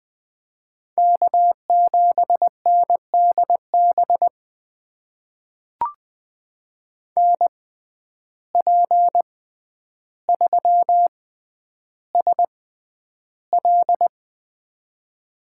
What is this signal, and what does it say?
Morse code, heard as a keyed tone, about 20 wpm, spaced as for 10 wpm, 700 Hz. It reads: K7NDB NP3SL
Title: USA Call Signs - 5 Characters - 6x Character Spacing - 20wpm